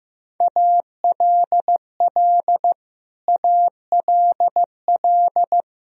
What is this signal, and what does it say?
Morse code, heard as a keyed tone, about 15 wpm, 700 Hz. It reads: ALL ALL
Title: Top 100 Words Repeated 15wpm